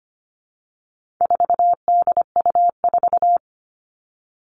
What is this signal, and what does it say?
Morse code, heard as a keyed tone, about 25 wpm, 700 Hz. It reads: 4DU4
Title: All Cut Numbers - 4-Digit 25wpm